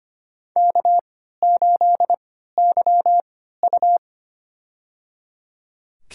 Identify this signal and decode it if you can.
Morse code, heard as a keyed tone, about 25 wpm, spaced as for 14 wpm, 700 Hz. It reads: K8YU
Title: USA Call Signs - 4 Characters - 3x Character Spacing - 25wpm